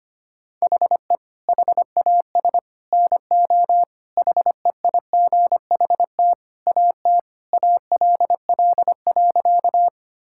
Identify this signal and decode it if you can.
Morse code, heard as a keyed tone, about 25 wpm, 700 Hz. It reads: HE HAS NO HEIGHT AT ALL.
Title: Sentences from Top 1600 Words 25wpm